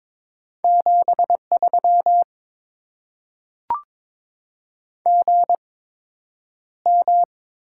Morse code, heard as a keyed tone, about 22 wpm, spaced as for 9 wpm, 700 Hz. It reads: 73 GM
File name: Top 20 QSO Elements - 8x Character Spacing - 22wpm